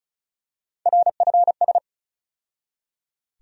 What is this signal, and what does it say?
Morse code, heard as a keyed tone, about 36 wpm, 700 Hz. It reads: RFS
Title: USA Call Sign Postfixes - 3 Letters 35wpm